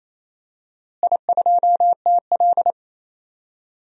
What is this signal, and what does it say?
Morse code, heard as a keyed tone, about 28 wpm, 700 Hz. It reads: I2TL
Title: DXCC-100 Call Signs - 4 Characters - Repeated 3x - 28wpm